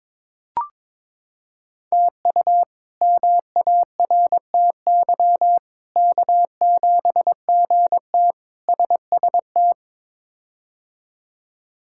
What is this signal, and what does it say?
Morse code, heard as a keyed tone, about 21 wpm, 700 Hz. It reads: TU MARTY K7GT SST
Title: K1USN SST QSO Practice - 22wpm